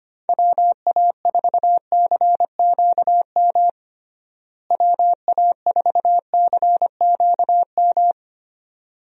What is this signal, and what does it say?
Morse code, heard as a keyed tone, about 25 wpm, 700 Hz. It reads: WA4CQM WA4CQM
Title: USA Call Signs - 6 Characters - Repeated 2x - 25wpm